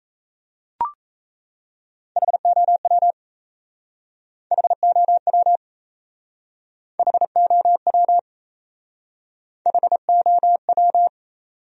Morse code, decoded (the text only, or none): HOW HOW HOW HOW